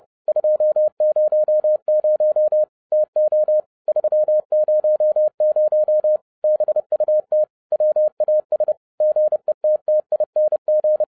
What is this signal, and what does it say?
Morse code, heard as a keyed tone, about 30 wpm, 610 Hz. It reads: 200 TO 300 BUT WAS GETTING